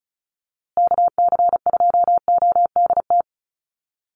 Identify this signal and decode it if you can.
Morse code, heard as a keyed tone, about 34 wpm, 700 Hz. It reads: KC2ODT
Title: USA Call Signs Repeated 2x 35wpm